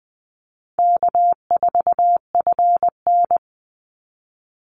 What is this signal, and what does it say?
Morse code, heard as a keyed tone, about 20 wpm, 700 Hz. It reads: K4FN
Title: Call Signs Repeated 2x - 20wpm